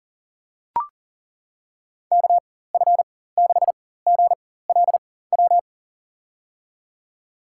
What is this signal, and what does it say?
Morse code, heard as a keyed tone, about 40 wpm, spaced as for 19 wpm, 700 Hz. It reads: KF6GLW
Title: USA Call Signs - 6 Characters - 4x Character Spacing - 40wpm